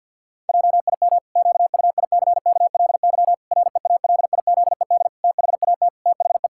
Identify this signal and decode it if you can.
Morse code, heard as a keyed tone, about 50 wpm, 700 Hz. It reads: JIM QUICKLY REALIZED THAT THE
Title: Pangram Challenge v2 - 50wpm